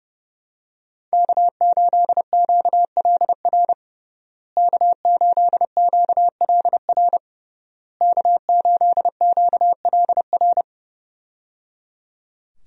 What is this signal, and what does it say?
Morse code, heard as a keyed tone, about 29 wpm, 700 Hz. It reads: K8QLR K8QLR K8QLR
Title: USA Call Signs - 5 Characters - Repeated 3x - 30wpm